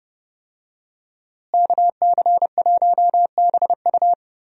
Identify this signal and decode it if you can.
Morse code, heard as a keyed tone, about 30 wpm, 700 Hz. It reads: KC1BU
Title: USA Call Signs - 5 Characters - Repeated 3x - 30wpm